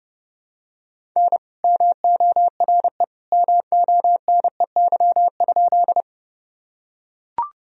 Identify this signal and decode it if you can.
Morse code, heard as a keyed tone, about 32 wpm, 700 Hz. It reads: N MORE MONEY?
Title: Sentences from Top 400 Words and Abbreviations 30wpm